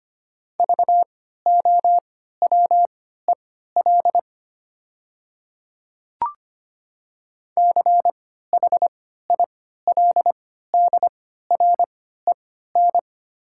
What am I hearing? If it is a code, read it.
Morse code, heard as a keyed tone, about 25 wpm, spaced as for 14 wpm, 700 Hz. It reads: VOWEL CHILDREN
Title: 3x Character Spacing - Top 500 Words - 25wpm